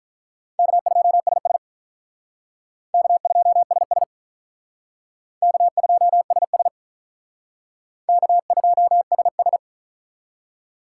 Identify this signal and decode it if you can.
Morse code, heard as a keyed tone, about 46 wpm, 700 Hz. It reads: K2SS K2SS K2SS K2SS